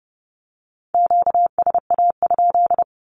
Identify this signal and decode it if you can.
Morse code, heard as a keyed tone, about 30 wpm, 700 Hz. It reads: QSA?
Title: Q-Codes with Spoken Meaning - 30wpm